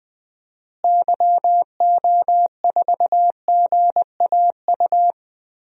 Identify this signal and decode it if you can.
Morse code, heard as a keyed tone, about 20 wpm, 700 Hz. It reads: YO4GAU
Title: DXCC-100 Call Signs - 6 Characters - Repeated 3x - 20wpm